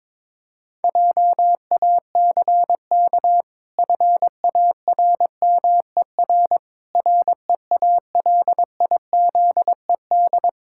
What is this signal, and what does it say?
Morse code, heard as a keyed tone, about 21 wpm, 700 Hz. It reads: JACK FARMER REALIZED
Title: Pangram Challenge v2 - 22wpm